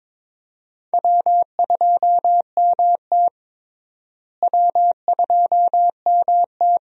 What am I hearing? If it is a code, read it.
Morse code, heard as a keyed tone, about 21 wpm, 700 Hz. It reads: W2MT W2MT
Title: USA Call Signs - 4 Characters - Repeated 2x - 22wpm